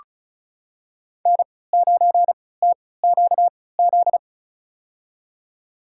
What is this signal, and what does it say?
Morse code, heard as a keyed tone, about 36 wpm, spaced as for 20 wpm, 700 Hz. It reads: N9TQZ
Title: USA Call Signs - 5 Characters - 3x Character Spacing - 35wpm